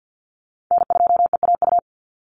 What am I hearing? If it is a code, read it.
Morse code, heard as a keyed tone, about 50 wpm, 700 Hz. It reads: N2EAU
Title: USA Call Signs 50wpm